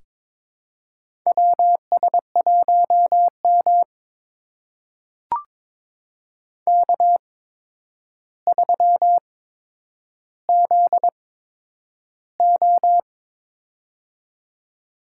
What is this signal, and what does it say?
Morse code, heard as a keyed tone, about 22 wpm, spaced as for 9 wpm, 700 Hz. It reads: WS1M K3ZO
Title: USA Call Signs - 4 Characters - 8x Character Spacing - 22wpm